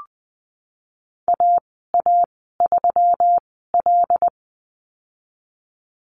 Morse code, heard as a keyed tone, about 20 wpm, spaced as for 15 wpm, 700 Hz. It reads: AA3L
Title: 140 Lesson 23g - Letter f - Call Signs 20wpm